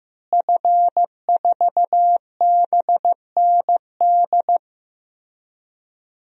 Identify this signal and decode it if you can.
Morse code, heard as a keyed tone, about 16 wpm, 700 Hz. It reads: F4BND
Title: DXCC-100 Call Signs - 5 Characters - Repeated 3x - 15wpm